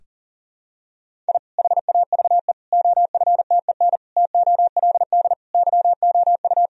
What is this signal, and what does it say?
Morse code, heard as a keyed tone, about 40 wpm, 700 Hz. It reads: I HAVE OFTEN TOLD YOU
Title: Sentences from Top 1000 Words - Review Part 1 - 40wpm